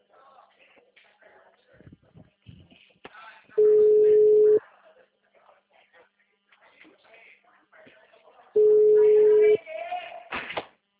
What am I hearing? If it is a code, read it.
Ringback tone (Europe, 425 Hz), with speech.